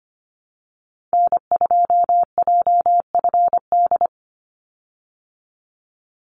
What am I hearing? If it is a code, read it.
Morse code, heard as a keyed tone, about 25 wpm, 700 Hz. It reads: N2JFD
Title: Call Signs Repeated 3x - 25wpm